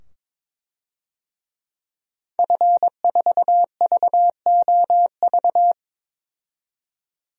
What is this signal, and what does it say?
Morse code, heard as a keyed tone, about 21 wpm, 700 Hz. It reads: F4VOV
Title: DXCC-100 Call Signs - 5 Characters - 22wpm